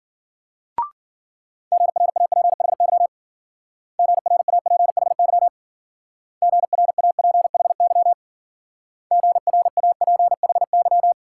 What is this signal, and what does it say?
Morse code, heard as a keyed tone, about 50 wpm, 700 Hz. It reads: GRAPHY GRAPHY GRAPHY GRAPHY
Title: Speed-Racing Multi-letter Phonemes - 40wpm